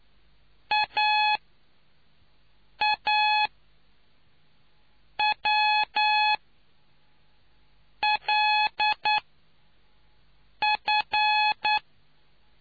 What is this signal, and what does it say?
Morse code, heard as a keyed tone, about 9 wpm, spaced as for 4 wpm, 850 Hz. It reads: AAWLF